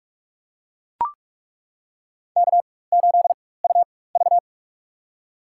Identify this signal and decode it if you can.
Morse code, heard as a keyed tone, about 42 wpm, spaced as for 21 wpm, 700 Hz. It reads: K8UV
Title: USA Call Signs - 4 Characters - 4x Character Spacing - 45wpm